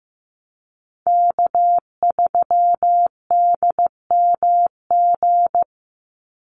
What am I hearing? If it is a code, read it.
Morse code, heard as a keyed tone, about 15 wpm, 700 Hz. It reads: K3DMG
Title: Call Signs Repeated 2x - 15wpm